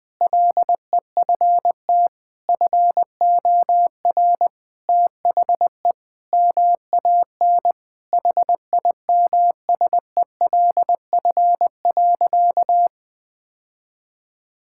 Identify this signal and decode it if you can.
Morse code, heard as a keyed tone, about 20 wpm, 700 Hz. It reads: LEFT FOR THE MAN HIMSELF.
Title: Sentences from Top 1600 Words 20wpm